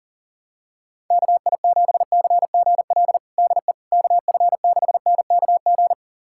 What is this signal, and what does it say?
Morse code, heard as a keyed tone, about 40 wpm, 700 Hz. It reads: KI7CGL DE KF6NKG
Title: USA Callsign DE Callsign - 6 Characters - 40wpm